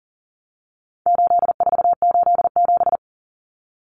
Morse code, heard as a keyed tone, about 37 wpm, 700 Hz. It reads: 8487
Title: Common Cut Numbers - 4-Digit 40wpm